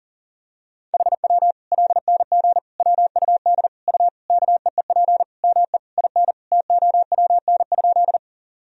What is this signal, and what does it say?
Morse code, heard as a keyed tone, about 40 wpm, 700 Hz. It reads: HW LNG WUD U KEEP ME IN TOWN?
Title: Sentences from Top 500 Words and Abbreviations 40wpm